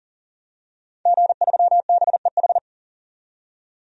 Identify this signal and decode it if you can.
Morse code, heard as a keyed tone, about 40 wpm, 700 Hz. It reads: G3BEH
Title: DXCC-100 Call Signs - 5 Characters - 40wpm